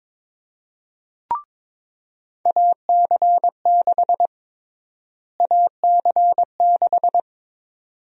Morse code, heard as a keyed tone, about 21 wpm, 700 Hz. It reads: AC6 AC6
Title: USA Call Sign Prefixes - Repeated 2x - 22wpm